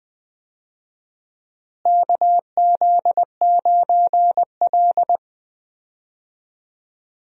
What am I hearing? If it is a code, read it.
Morse code, heard as a keyed tone, about 20 wpm, 700 Hz. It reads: KZ9L
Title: USA Call Signs - 4 Characters - Repeated 2x - 20wpm